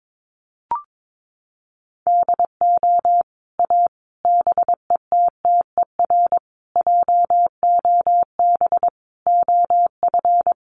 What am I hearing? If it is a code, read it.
Morse code, heard as a keyed tone, about 22 wpm, 700 Hz. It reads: DO A BETTER JOB OF
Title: Sets of 5 Words Encore2 22wpm